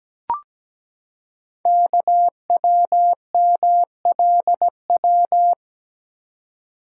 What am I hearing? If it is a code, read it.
Morse code, heard as a keyed tone, about 17 wpm, 700 Hz. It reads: KWMLW
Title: Code Groups - 5 Letters - 17wpm